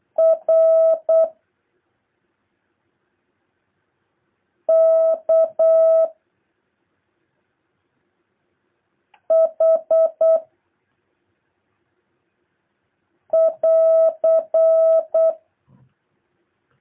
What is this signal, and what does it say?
Morse code, heard as a keyed tone, about 8 wpm, 650 Hz. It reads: R K H <AR>
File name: jh8cbh / 2023cw